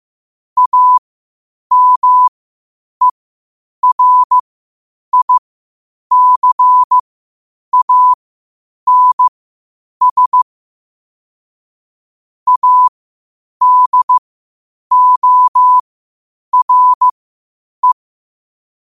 Morse code, heard as a keyed tone, about 15 wpm, spaced as for 8 wpm, 1000 Hz. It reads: AMERICANS ADORE
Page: Quotes for Fri, 15 Aug 2025 in Morse Code at 8 words per minute.